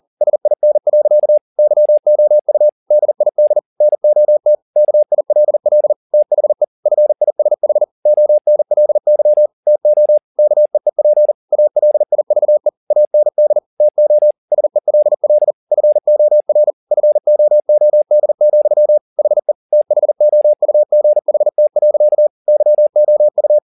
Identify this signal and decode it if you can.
Morse code, heard as a keyed tone, about 40 wpm, 600 Hz. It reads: SIN. YOU DID NOT KILL THE FISH ONLY TO KEEP ALIVE AND TO SELL FOR FOOD, HE THOUGHT. YOU
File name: the old man and the sea 2 40wpm